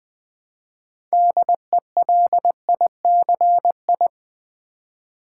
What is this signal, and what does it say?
Morse code, heard as a keyed tone, about 20 wpm, 700 Hz. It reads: DELICI
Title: Build A Word - 3 Letters At A Time - 9-Letter Words - 20wpm